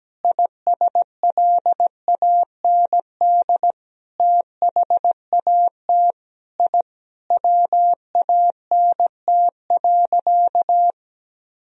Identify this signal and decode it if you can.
Morse code, heard as a keyed tone, about 17 wpm, 700 Hz. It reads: ISLAND THAT I WANT.
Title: Sentences from Top 1000 Words - Review Part 3 - 17wpm